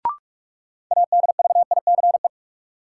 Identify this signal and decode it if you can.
Morse code, heard as a keyed tone, about 46 wpm, 700 Hz. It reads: ADVICE
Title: Sentences from Top 1100 Words - New Word Drill - 45wpm